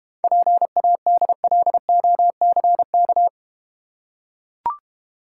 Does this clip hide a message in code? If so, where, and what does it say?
Yes: Morse code from 0.2 to 3.3 s, heard as a keyed tone, about 32 wpm, 700 Hz. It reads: PADLOCK